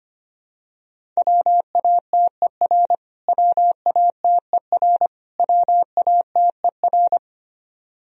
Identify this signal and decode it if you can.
Morse code, heard as a keyed tone, about 25 wpm, 700 Hz. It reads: WATER WATER WATER
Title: Top 100 Words Repeated 25wpm